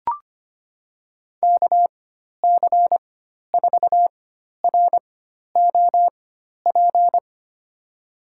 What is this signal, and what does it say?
Morse code, heard as a keyed tone, about 25 wpm, spaced as for 12 wpm, 700 Hz. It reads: KC4ROP
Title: USA Call Signs - 6 Characters - 4x Character Spacing - 25wpm